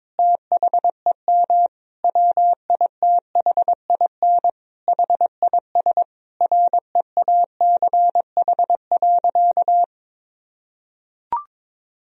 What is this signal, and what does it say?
Morse code, heard as a keyed tone, about 22 wpm, 700 Hz. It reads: THEM WITHIN HIS REACH.
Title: Sentences from Top 1000 Words - Review Part 4 - 22wpm